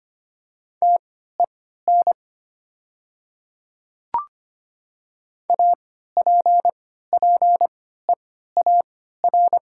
Morse code, heard as a keyed tone, about 25 wpm, spaced as for 14 wpm, 700 Hz. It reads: TEN APPEAR
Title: 3x Character Spacing - Top 500 Words - 25wpm